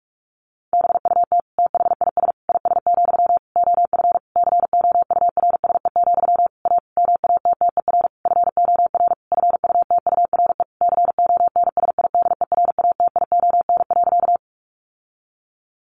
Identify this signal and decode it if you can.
Morse code, heard as a keyed tone, about 44 wpm, 700 Hz. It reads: BUT THIS IS, OF COURSE, A MATTER FOR FUTURE CONSIDERATION.
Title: Sentences from Top 1500 Words 45wpm